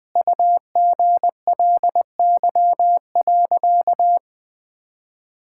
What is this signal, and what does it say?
Morse code, heard as a keyed tone, about 20 wpm, 700 Hz. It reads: UGLY.
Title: Sentences with 7-Word Phrases - Spoken Beginning - 20wpm